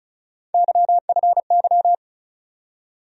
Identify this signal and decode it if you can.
Morse code, heard as a keyed tone, about 34 wpm, 700 Hz. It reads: YFY
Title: USA Call Sign Postfixes - 3 Letters 35wpm